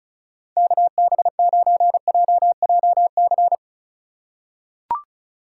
Morse code, heard as a keyed tone, about 34 wpm, 700 Hz. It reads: KB9JJC
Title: USA Call Signs - 6 Characters - 4x Character Spacing - 35wpm